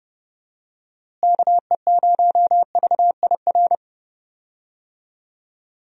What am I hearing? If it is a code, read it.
Morse code, heard as a keyed tone, about 30 wpm, 700 Hz. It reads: KE0VIR
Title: USA Call Signs - 6 Characters - Repeated 3x - 30wpm